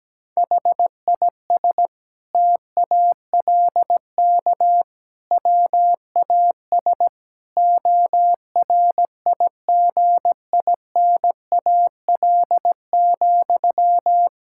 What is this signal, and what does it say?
Morse code, heard as a keyed tone, about 17 wpm, 700 Hz. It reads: HIS TALK WAS ORIGINAL,
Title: Sentences from Top 1000 Words - Review Part 6 - 17wpm